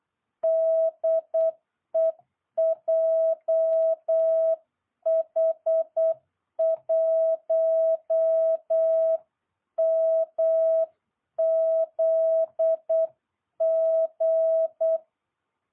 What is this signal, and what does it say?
Morse code, heard as a keyed tone, about 8 wpm, 660 Hz. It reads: DE JH1MZG